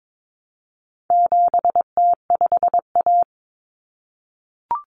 Morse code, heard as a keyed tone, about 22 wpm, 700 Hz. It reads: 7T5A
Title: Common Cut Numbers - 4-Digit 22wpm